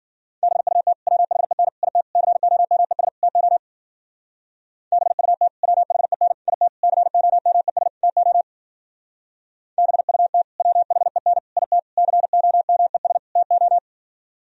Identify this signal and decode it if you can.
Morse code, heard as a keyed tone, about 50 wpm, 700 Hz. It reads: BUT WHEN IT COMES TO BUT WHEN IT COMES TO BUT WHEN IT COMES TO
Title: Speed-Racing - Sets of 5 Words 40wpm